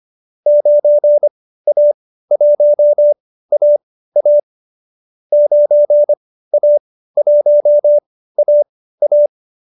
Morse code, heard as a keyed tone, about 25 wpm, spaced as for 15 wpm, 590 Hz. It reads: 9A1AA 9A1AA